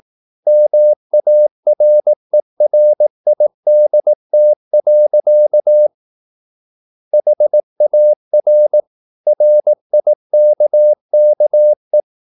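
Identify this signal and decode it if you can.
Morse code, heard as a keyed tone, about 18 wpm, 600 Hz. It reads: MARERIDT. HAR RIKKE
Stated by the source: Slow (QRS) Morse practice at 18 wpm, CW med Gnister.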